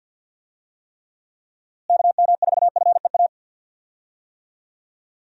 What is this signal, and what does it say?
Morse code, heard as a keyed tone, about 52 wpm, 700 Hz. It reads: KM4FEA
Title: USA Call Signs - 6 Characters - Repeated 3x - 50wpm